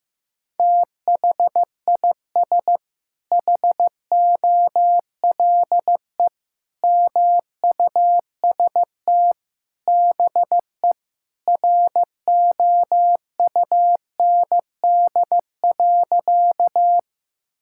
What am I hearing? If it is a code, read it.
Morse code, heard as a keyed tone, about 15 wpm, 700 Hz. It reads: THIS HOLE MUST BE ROUND.
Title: Sentences from Top 1600 Words 15wpm